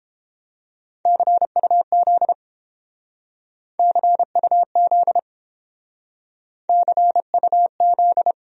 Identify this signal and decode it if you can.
Morse code, heard as a keyed tone, about 30 wpm, 700 Hz. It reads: CUZ CUZ CUZ